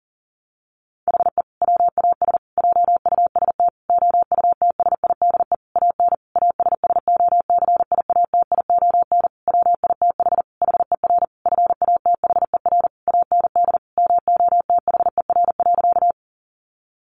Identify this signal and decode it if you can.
Morse code, heard as a keyed tone, about 40 wpm, 700 Hz. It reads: HE WAS JUST OUTSIDE AN ASSOCIATION WITH HER FATHER AND MOTHER.
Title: Sentences from Top 1100 Words 40wpm